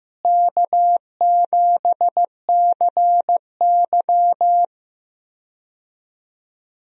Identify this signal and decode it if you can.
Morse code, heard as a keyed tone, about 15 wpm, 700 Hz. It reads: K7CY
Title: USA Call Signs - 4 Characters - Repeated 3x - 15wpm